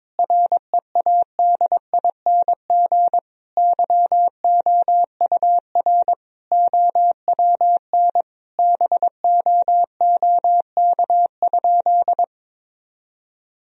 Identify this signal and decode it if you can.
Morse code, heard as a keyed tone, about 22 wpm, 700 Hz. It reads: READING YOUR OWN BOOK?
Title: Sentences from Top 1000 Words - Review Part 3 - 22wpm